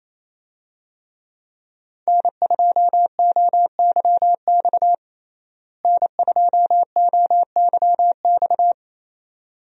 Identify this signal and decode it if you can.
Morse code, heard as a keyed tone, about 28 wpm, 700 Hz. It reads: N2OYX N2OYX
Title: USA Call Signs - 5 Characters - Repeated 2x - 28wpm